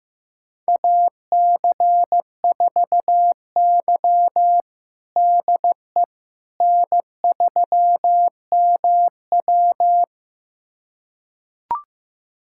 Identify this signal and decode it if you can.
Morse code, heard as a keyed tone, about 16 wpm, 700 Hz. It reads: AC4Y DE N3MW
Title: USA Callsign DE Callsign - 4 Characters - 15wpm